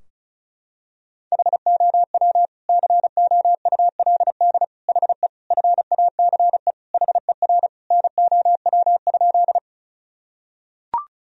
Morse code, heard as a keyed tone, about 36 wpm, 700 Hz. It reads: HOW COULD HE FACE HER NOW?
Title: Sentences from Top 1000 Words - Review Part 2 - 35wpm